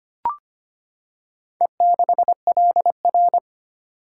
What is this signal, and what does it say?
Morse code, heard as a keyed tone, about 25 wpm, 690 Hz. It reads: E6LR
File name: practice / 25wpm / Four Letter-Number Combinations - 25wpm